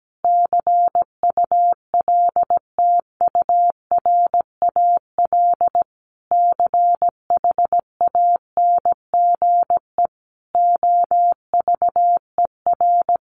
Morse code, heard as a keyed tone, about 17 wpm, 700 Hz. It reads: CULTURAL CHANGE OVER
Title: Sentences from Top 1500 Words 17wpm